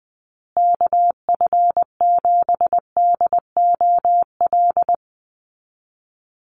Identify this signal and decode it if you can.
Morse code, heard as a keyed tone, about 20 wpm, 700 Hz. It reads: KF7DOL
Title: USA Call Signs Repeated 2x 20wpm